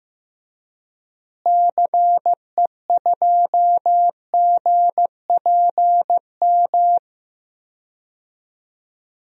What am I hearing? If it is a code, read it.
Morse code, heard as a keyed tone, about 15 wpm, 700 Hz. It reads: CE2GPM
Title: DXCC-100 Call Signs - 6 Characters - Repeated 3x - 15wpm